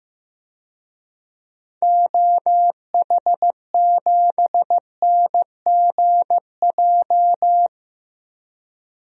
Morse code, heard as a keyed tone, about 15 wpm, 700 Hz. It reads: OH7NGJ
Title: DXCC-100 Call Signs - 6 Characters - Repeated 3x - 15wpm